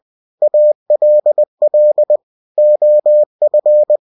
Morse code, heard as a keyed tone, about 20 wpm, 600 Hz. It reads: ALL OF
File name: cw-books / winnie-the-pooh / winnie-20wpm-page-0084